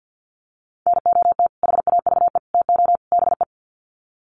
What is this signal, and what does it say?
Morse code, heard as a keyed tone, about 48 wpm, 700 Hz. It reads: NOT HAVE TO BE
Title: Sets of 4 Words 50wpm